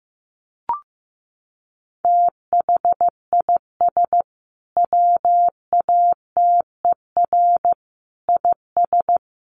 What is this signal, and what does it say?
Morse code, heard as a keyed tone, about 15 wpm, 700 Hz. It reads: THIS WATER IS
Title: Sentences from Top 1300 Words 15wpm